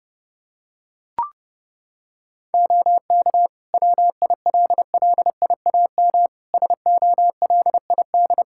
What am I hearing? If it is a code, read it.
Morse code, heard as a keyed tone, about 32 wpm, 700 Hz. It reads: OK WILLIAM SOLID
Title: Practice QSO - Exchange 3 - More Info - 30wpm